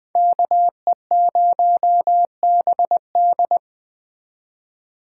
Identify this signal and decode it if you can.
Morse code, heard as a keyed tone, about 20 wpm, 700 Hz. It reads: KE0BD